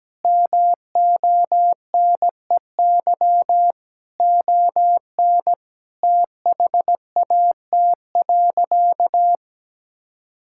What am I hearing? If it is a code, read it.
Morse code, heard as a keyed tone, about 17 wpm, 700 Hz. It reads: MONEY ON THAT.
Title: Sentences from Top 1600 Words 17wpm